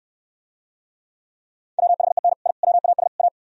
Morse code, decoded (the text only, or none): WHEN I WAS A